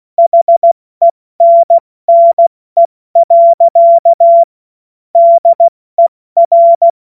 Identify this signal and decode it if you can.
Morse code, heard as a keyed tone, about 16 wpm, 680 Hz. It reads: HENNE. DER
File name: Witz-15-Juli-2025-70-F80